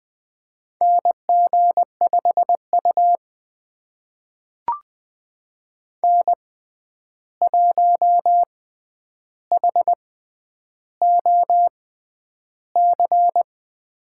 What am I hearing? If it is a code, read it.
Morse code, heard as a keyed tone, about 20 wpm, spaced as for 10 wpm, 700 Hz. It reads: NG5U N1HOC
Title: USA Call Signs - 6x Character Spacing - 20wpm